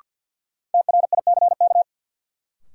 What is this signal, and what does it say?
Morse code, heard as a keyed tone, about 52 wpm, 700 Hz. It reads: TRICK
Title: Sentences from Top 1500 Words - New Word Drill Alt - 50wpm